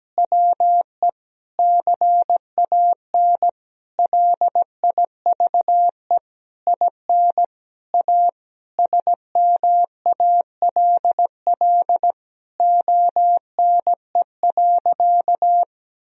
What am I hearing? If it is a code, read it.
Morse code, heard as a keyed tone, about 17 wpm, 700 Hz. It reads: WE CAN LIVE IN A SMALL ONE.
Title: Sentences from Top 1000 Words - Review Part 1 - 17wpm